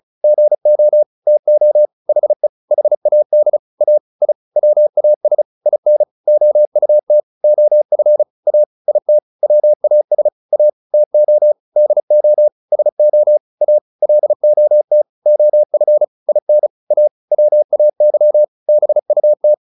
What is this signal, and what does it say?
Morse code, heard as a keyed tone, about 34 wpm, 600 Hz. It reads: GO TO HE HAD A I WAS IN OUT OF A IT WAS A TO DO SO A LOT OF IN A WAY BUT
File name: WD201 35